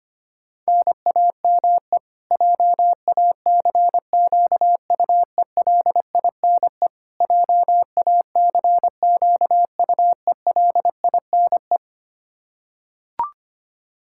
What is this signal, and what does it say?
Morse code, heard as a keyed tone, about 25 wpm, 700 Hz. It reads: NAME JACQUELINE JACQUELINE
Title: Practice QSO - Exchange 1 - Beginning - 25wpm